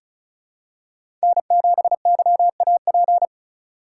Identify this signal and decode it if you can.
Morse code, heard as a keyed tone, about 36 wpm, 700 Hz. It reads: N7YAP
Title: USA Call Signs - 5 Characters - 2x Character Spacing - 35wpm